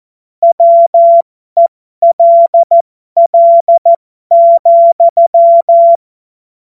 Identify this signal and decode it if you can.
Morse code, heard as a keyed tone, about 14 wpm, spaced as for 12 wpm, 680 Hz. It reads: WELL,